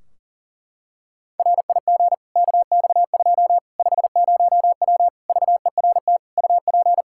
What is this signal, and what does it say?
Morse code, heard as a keyed tone, about 40 wpm, 700 Hz. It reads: RIG KX2 50W VERT UP
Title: Practice QSO - Exchange 2 - Info - 40wpm